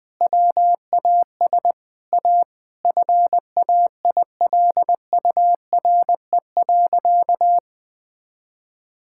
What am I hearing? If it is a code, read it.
Morse code, heard as a keyed tone, about 20 wpm, 700 Hz. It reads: WAS A FAILURE.
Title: Sentences from Top 1000 Words - Review Part 6 - 20wpm